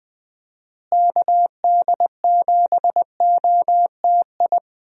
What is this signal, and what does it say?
Morse code, heard as a keyed tone, about 20 wpm, 700 Hz. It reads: KD7OTI
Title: USA Call Signs - 6 Characters - Repeated 2x - 20wpm